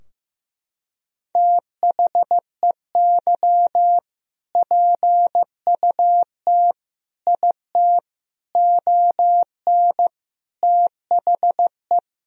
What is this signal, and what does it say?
Morse code, heard as a keyed tone, about 15 wpm, 700 Hz. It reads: THEY PUT IT ON THE
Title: Sentences from Top 1000 Words - Review Part 4 - 15wpm